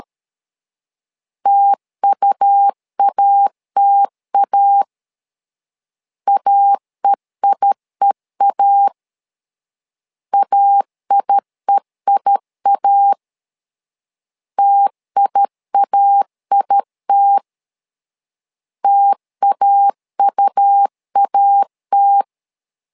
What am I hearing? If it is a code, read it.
Morse code, heard as a keyed tone, about 13 wpm, 800 Hz. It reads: TUATA AEIEA AIEIA TIAIT TAUAT
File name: LEZIONI / 2 UHDN / 2 U 12wpm 1338